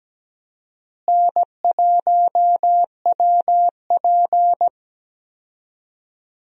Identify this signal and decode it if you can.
Morse code, heard as a keyed tone, about 17 wpm, 700 Hz. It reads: N1WP
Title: USA Call Signs - 4 Characters - Repeated 3x - 17wpm